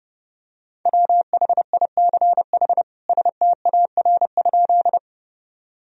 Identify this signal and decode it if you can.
Morse code, heard as a keyed tone, about 30 wpm, 700 Hz. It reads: WHICH STAR?
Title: Sentences from Top 1000 Words - Review Part 2 - 30wpm